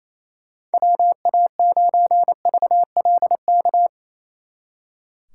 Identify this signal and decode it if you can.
Morse code, heard as a keyed tone, about 34 wpm, 700 Hz. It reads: WA9VLK
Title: Speed Racing - USA 2x3 Call Signs 28wpm